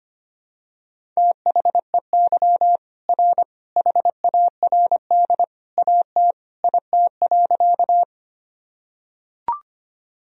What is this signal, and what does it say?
Morse code, heard as a keyed tone, about 25 wpm, 700 Hz. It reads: THEY R HARD AT IT.
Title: Sentences from Top 300 Words and Abbreviations 25wpm